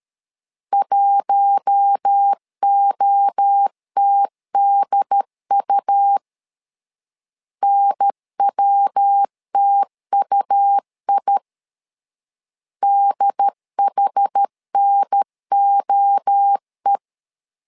Morse code, heard as a keyed tone, about 13 wpm, 800 Hz. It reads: 1OTDU NWTUI DHNOE
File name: LEZIONI / 4 Q1S 12wpm 1802